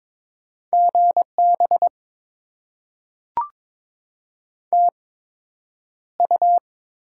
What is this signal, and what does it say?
Morse code, heard as a keyed tone, about 22 wpm, spaced as for 9 wpm, 700 Hz. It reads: GB TU
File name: Top 90 QSO Elements - 8x Character Spacing - 22wpm